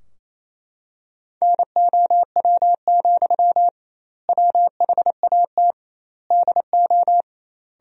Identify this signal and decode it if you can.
Morse code, heard as a keyed tone, about 28 wpm, 700 Hz. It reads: NOW, WHAT DO
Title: 2x Word-Spacing - Sentences from Top 200 Words - 28wpm